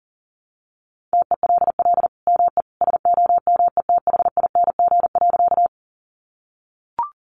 Morse code, heard as a keyed tone, about 40 wpm, 700 Hz. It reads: TELL ME SOMETHING.
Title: Sentences from Top 600 Words 40wpm